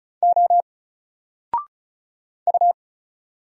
Morse code, heard as a keyed tone, about 35 wpm, 700 Hz. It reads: O U
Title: Single Character (Letter:Number:Punctuation:Prosign) 35wpm